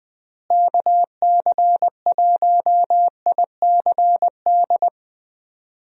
Morse code, heard as a keyed tone, about 20 wpm, 700 Hz. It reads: KC1ICD
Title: USA Call Signs - 6 Characters - Repeated 2x - 20wpm